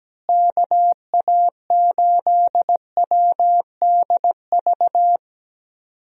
Morse code, heard as a keyed tone, about 17 wpm, 700 Hz. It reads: KA8WDV
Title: USA Call Signs - 6 Characters - Repeated 3x - 17wpm